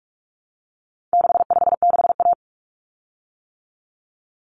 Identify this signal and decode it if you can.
Morse code, heard as a keyed tone, about 44 wpm, 700 Hz. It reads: 656A
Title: Common Cut Numbers - 4-Digit 45wpm